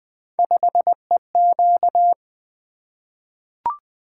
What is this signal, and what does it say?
Morse code, heard as a keyed tone, about 20 wpm, 700 Hz. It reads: 5EQ
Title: Three Letter-Number Combinations - 20wpm